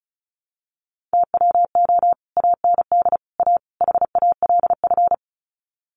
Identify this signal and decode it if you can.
Morse code, heard as a keyed tone, about 34 wpm, 700 Hz. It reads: TWO AND A HALF
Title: Sets of 4 Words 35wpm